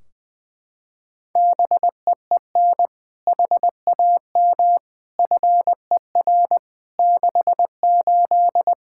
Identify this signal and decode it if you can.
Morse code, heard as a keyed tone, about 21 wpm, 700 Hz. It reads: BEEN HAM FER 68
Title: Practice QSO - Exchange 3 - More Info - 20wpm